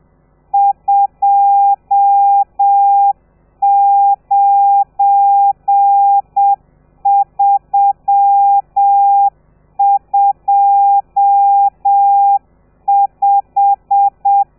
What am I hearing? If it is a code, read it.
Morse code, heard as a keyed tone, about 7 wpm, 800 Hz. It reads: 29325
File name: telegrafia / i2tfj / Lez12 30